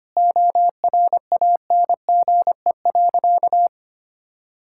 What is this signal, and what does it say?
Morse code, heard as a keyed tone, about 25 wpm, 700 Hz. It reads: ORANGE.
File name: Easy Sentences with Hints - 25wpm